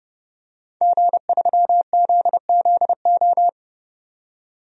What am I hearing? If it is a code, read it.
Morse code, heard as a keyed tone, about 30 wpm, 700 Hz. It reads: G3ZZO
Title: DXCC-100 Call Signs - 5 Characters - 30wpm